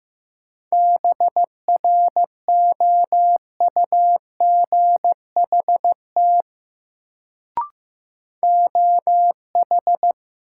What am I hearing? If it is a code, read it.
Morse code, heard as a keyed tone, about 15 wpm, 700 Hz. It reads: BROUGHT OH
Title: Sentences from Top 500 Words - New Word Drill - 15wpm